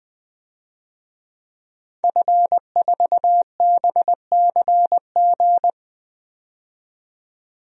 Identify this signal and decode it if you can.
Morse code, heard as a keyed tone, about 21 wpm, 700 Hz. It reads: F4BCG
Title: DXCC-100 Call Signs - 5 Characters - 20wpm